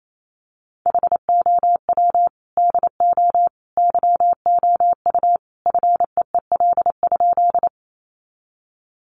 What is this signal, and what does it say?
Morse code, heard as a keyed tone, about 28 wpm, 700 Hz. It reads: HOW DO YOU FEEL?
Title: Sentences from Top 400 Words 28wpm